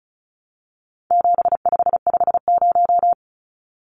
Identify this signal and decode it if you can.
Morse code, heard as a keyed tone, about 34 wpm, 700 Hz. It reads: 7550
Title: Numbers - 4-Digit 35wpm